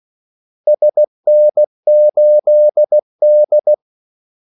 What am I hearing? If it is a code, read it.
Morse code, heard as a keyed tone, about 16 wpm, 600 Hz. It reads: SN8D
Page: Kaldesignaler 16 wpm